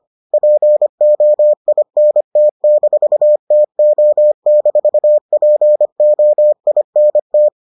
Morse code, heard as a keyed tone, about 25 wpm, 600 Hz. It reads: POINT-TO-POINT